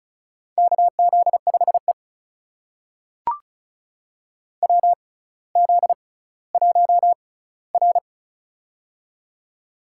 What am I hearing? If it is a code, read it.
Morse code, heard as a keyed tone, about 36 wpm, spaced as for 18 wpm, 700 Hz. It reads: KZ5E WZ1R
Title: USA Call Signs - 4 Characters - 6x Character Spacing - 35wpm